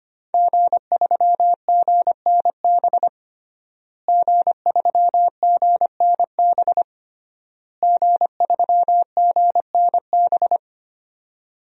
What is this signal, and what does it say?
Morse code, heard as a keyed tone, about 25 wpm, 700 Hz. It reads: G3GNB G3GNB G3GNB
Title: DXCC-100 Call Signs - 5 Characters - Repeated 3x - 25wpm